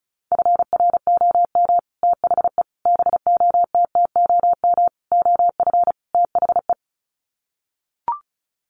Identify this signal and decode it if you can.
Morse code, heard as a keyed tone, about 34 wpm, 700 Hz. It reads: FROM THE BOTTOM OF THE
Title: Sets of 5 Words Encore2 35wpm